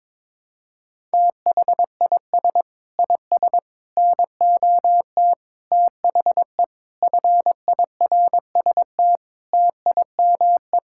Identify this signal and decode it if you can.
Morse code, heard as a keyed tone, about 22 wpm, 700 Hz. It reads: THIS IS NOT THE FIRST TIME
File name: practice / 22wpm / Sets of 8 Words - 22wpm